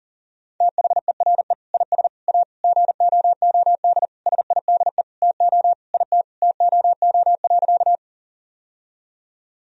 Morse code, heard as a keyed tone, about 40 wpm, 700 Hz. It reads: THERE IS A GOOD SIDE TO IT TOO.
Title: Sentences from Top 1000 Words - Review Part 1 - 40wpm